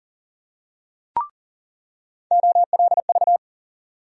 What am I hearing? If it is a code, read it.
Morse code, heard as a keyed tone, about 40 wpm, 700 Hz. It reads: OLV
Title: USA Call Sign Postfixes - 3 Letters 40wpm